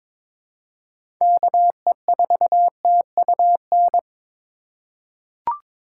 Morse code, heard as a keyed tone, about 22 wpm, 700 Hz. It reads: KE4TUN
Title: USA Call Signs - 6 Characters - 2x Character Spacing - 22wpm